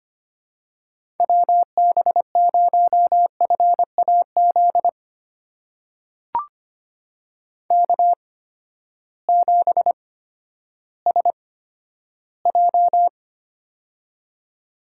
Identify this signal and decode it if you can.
Morse code, heard as a keyed tone, about 25 wpm, spaced as for 12 wpm, 700 Hz. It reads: WB0FAZ K7SJ
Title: USA Call Signs - 8x Character Spacing - 25wpm